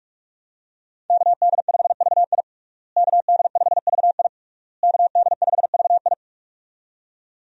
Morse code, heard as a keyed tone, about 46 wpm, 700 Hz. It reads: KD5VI KD5VI KD5VI
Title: USA Call Signs - 5 Characters - Repeated 3x - 45wpm